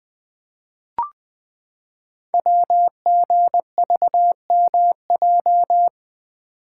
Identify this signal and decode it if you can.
Morse code, heard as a keyed tone, about 20 wpm, 700 Hz. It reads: WGVMJ
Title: Code Groups - 5 Letters - 20wpm